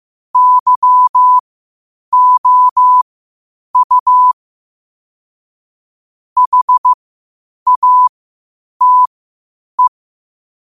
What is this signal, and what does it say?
Morse code, heard as a keyed tone, about 15 wpm, spaced as for 8 wpm, 1000 Hz. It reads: YOU HATE
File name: qotdfiles / 8wpm